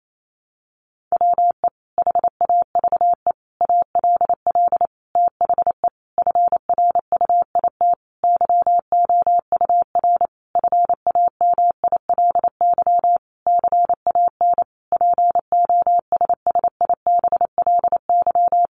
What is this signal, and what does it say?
Morse code, heard as a keyed tone, about 28 wpm, 700 Hz. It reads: WE HAVE ALL THE FRUIT YOUR FAMILY CAN POSSIBLY
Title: Sentences from Top 1300 Words 28wpm